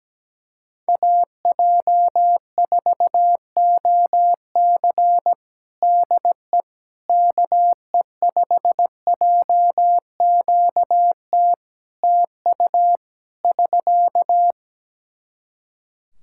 Morse code, heard as a keyed tone, about 17 wpm, 700 Hz. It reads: AJ4OC DE KE5JQT TU <SK>
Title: K1USN SST QSO Practice 17wpm